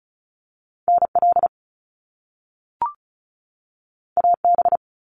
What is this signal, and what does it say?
Morse code, heard as a keyed tone, about 36 wpm, 700 Hz. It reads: NL AB
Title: Canadian Provinces and Territories 35wpm